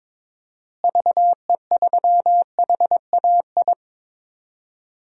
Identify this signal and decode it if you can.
Morse code, heard as a keyed tone, about 22 wpm, 700 Hz. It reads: VE3HAI
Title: DXCC-100 Call Signs - 6 Characters - Repeated 2x - 22wpm